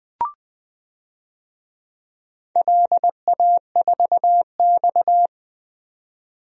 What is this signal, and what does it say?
Morse code, heard as a keyed tone, about 20 wpm, 700 Hz. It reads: LA4X
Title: DXCC-100 Call Signs - 4 Characters - 20wpm